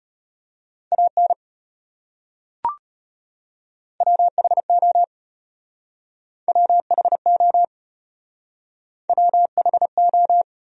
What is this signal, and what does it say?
Morse code, heard as a keyed tone, about 33 wpm, 700 Hz. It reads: AN WHO WHO WHO